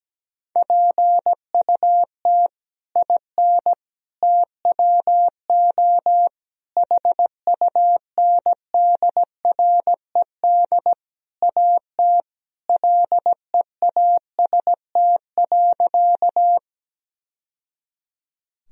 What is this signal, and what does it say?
Morse code, heard as a keyed tone, about 17 wpm, 700 Hz. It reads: PUT IN TWO HUNDRED AT LEAST.
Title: Sentences from Top 1000 Words - Review Part 2 - 17wpm